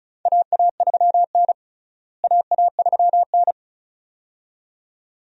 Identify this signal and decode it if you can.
Morse code, heard as a keyed tone, about 36 wpm, 700 Hz. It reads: AA3N AA3N
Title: USA Call Signs - 4 Characters - Repeated 2x - 35wpm